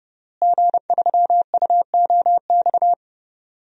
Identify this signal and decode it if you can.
Morse code, heard as a keyed tone, about 30 wpm, 700 Hz. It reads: G3UOX
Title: DXCC-100 Call Signs - 5 Characters - Repeated 2x - 30wpm